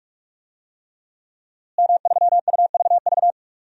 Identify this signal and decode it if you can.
Morse code, heard as a keyed tone, about 44 wpm, 700 Hz. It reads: M3UVV
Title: DXCC-100 Call Signs - 5 Characters - Repeated 3x - 45wpm